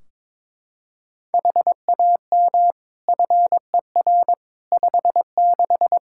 Morse code, heard as a keyed tone, about 22 wpm, 700 Hz. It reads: HAM FER 56
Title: Practice QSO - Exchange 3 - More Info - 22wpm